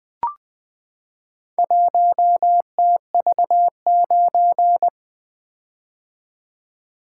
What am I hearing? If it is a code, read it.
Morse code, heard as a keyed tone, about 20 wpm, 700 Hz. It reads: 1TV9
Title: Four Letter-Number Combinations - 20wpm